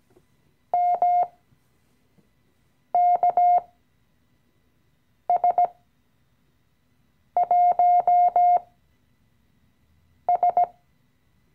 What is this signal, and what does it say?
Morse code, heard as a keyed tone, about 17 wpm, spaced as for 5 wpm, 710 Hz. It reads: MKS1S
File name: CW-Online-Kurs / Anfänger / Lektion 017